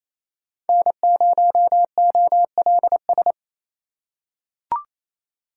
Morse code, heard as a keyed tone, about 28 wpm, 700 Hz. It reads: N0OLS
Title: USA Call Signs - 5 Characters - 4x Character Spacing - 28wpm